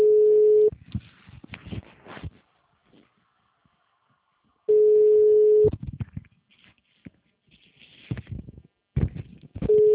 Ringback tone (Europe, 425 Hz).
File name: out- -2048- - - .67687